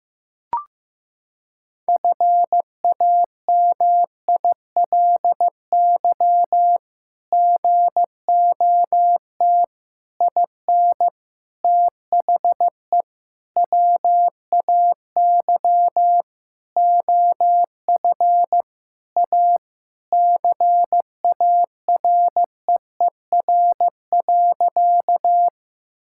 Morse code, heard as a keyed tone, about 16 wpm, 700 Hz. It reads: FAMILY GOT IN THE WAY OF A CAREER.
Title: Sentences from Top 1000 Words - Review Part 6 - 15wpm